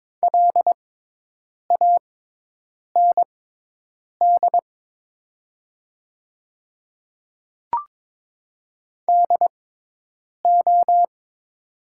Morse code, heard as a keyed tone, about 22 wpm, spaced as for 8 wpm, 700 Hz. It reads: LAND DO
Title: 6x Character Spacing - Top 200 Words - 22wpm